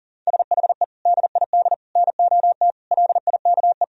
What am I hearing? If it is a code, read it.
Morse code, heard as a keyed tone, about 37 wpm, 700 Hz. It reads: SHE DID NOT LIKE
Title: Sentences from Top 1000 Words - Review Part 4 - 40wpm